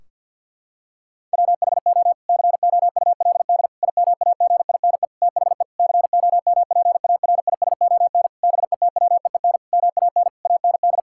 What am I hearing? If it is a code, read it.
Morse code, heard as a keyed tone, about 50 wpm, 700 Hz. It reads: WHO COULD IMAGINE THE COMPARISON BETWEEN MAN AND
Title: Sentences from Top 1600 Words 50wpm